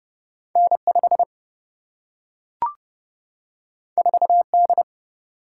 Morse code, heard as a keyed tone, about 30 wpm, 700 Hz. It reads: N 5 4 D
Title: Single-Letter Single-Number 30wpm